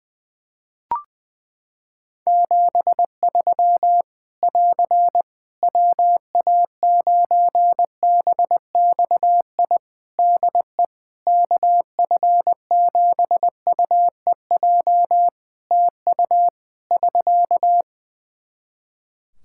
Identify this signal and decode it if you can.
Morse code, heard as a keyed tone, about 21 wpm, 700 Hz. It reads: 73 <AR> WA9BXI DE KF7UEJ TU <SK>
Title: K1USN SST QSO Practice 20wpm